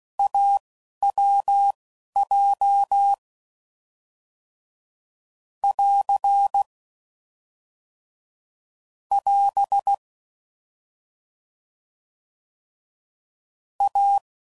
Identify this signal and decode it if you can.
Morse code, heard as a keyed tone, about 16 wpm, spaced as for 11 wpm, 800 Hz. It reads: AWJ <AR> <AS> A
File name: Combinations / Difficult Combos / AWJ1